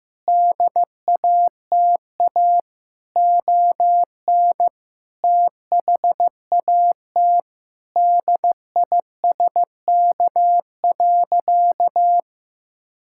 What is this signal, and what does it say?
Morse code, heard as a keyed tone, about 15 wpm, 700 Hz. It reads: DATA ON THAT DISK.
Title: Sentences from Top 1000 Words - Review Part 6 - 15wpm